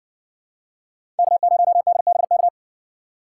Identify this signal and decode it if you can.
Morse code, heard as a keyed tone, about 57 wpm, 700 Hz. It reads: K0DBK